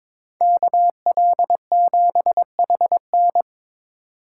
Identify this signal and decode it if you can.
Morse code, heard as a keyed tone, about 27 wpm, 700 Hz. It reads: KL7HN
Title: Speed Racing - USA 2x2 Call Signs 22wpm